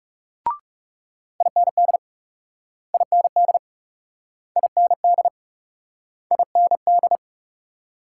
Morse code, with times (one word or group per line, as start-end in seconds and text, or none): 1.4-2.0 IND
2.9-3.6 IND
4.6-5.3 IND
6.3-7.2 IND